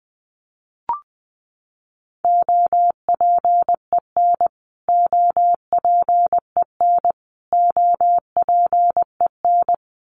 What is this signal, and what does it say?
Morse code, heard as a keyed tone, about 20 wpm, 700 Hz. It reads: OPEN OPEN OPEN
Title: Top 300 Words Repeated 20wpm